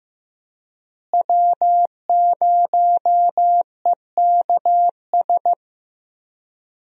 Morse code, heard as a keyed tone, about 15 wpm, 700 Hz. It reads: W0EKS
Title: USA Call Signs - 5 Characters - Repeated 3x - 15wpm